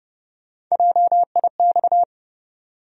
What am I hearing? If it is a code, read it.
Morse code, heard as a keyed tone, about 30 wpm, 700 Hz. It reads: JIX
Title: USA Call Sign Postfixes - 3 Letters 30wpm